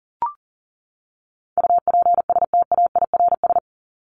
Morse code, heard as a keyed tone, about 40 wpm, 700 Hz. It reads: UPSTAIRS
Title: Top 1751-2000 Words 40wpm